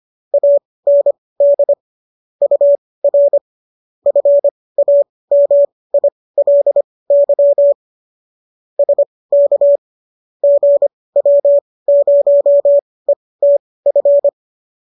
Morse code, heard as a keyed tone, about 25 wpm, spaced as for 18 wpm, 570 Hz. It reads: AND UR FAMILY SK GW0ETF
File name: qso218 18